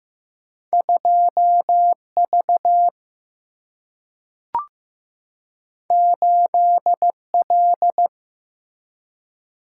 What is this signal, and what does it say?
Morse code, heard as a keyed tone, about 15 wpm, 700 Hz. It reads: 2V 8L
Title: Single-Letter Single-Number 15wpm